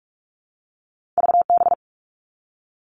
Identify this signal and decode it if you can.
Morse code, heard as a keyed tone, about 46 wpm, 700 Hz. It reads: VB
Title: All Cut Numbers - 2-Digit 45wpm